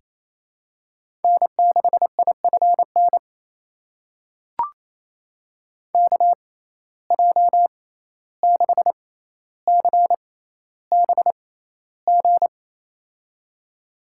Morse code, heard as a keyed tone, about 28 wpm, spaced as for 14 wpm, 700 Hz. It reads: N6IFN KJ6CBG
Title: USA Call Signs - 6x Character Spacing - 28wpm